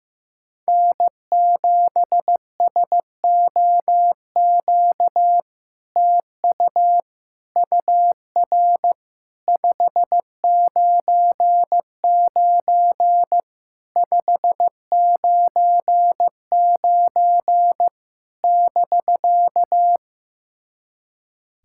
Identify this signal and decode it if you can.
Morse code, heard as a keyed tone, about 15 wpm, 700 Hz. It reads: N7SOQ TU UR 599 599 <BK>
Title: SOTA QSO Practice 15wpm